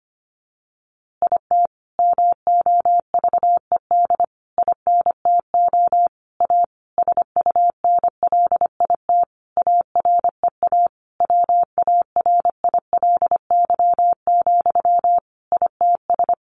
Morse code, heard as a keyed tone, about 25 wpm, 700 Hz. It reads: IT MOVED INTO A SUNLIT AREA WARILY, ITS
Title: A Sound Of Thunder 25wpm